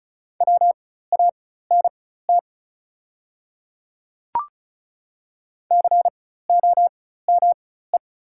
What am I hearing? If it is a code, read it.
Morse code, heard as a keyed tone, about 36 wpm, spaced as for 16 wpm, 690 Hz. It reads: WANT COME
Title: 4x Character Spacing - Top 200 Words - 35wpm